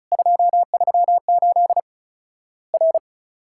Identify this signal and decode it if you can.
Morse code, heard as a keyed tone, about 36 wpm, 700 Hz. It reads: 238 R
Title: Worked All Europe DX Contest - QTC Exchange - 35wpm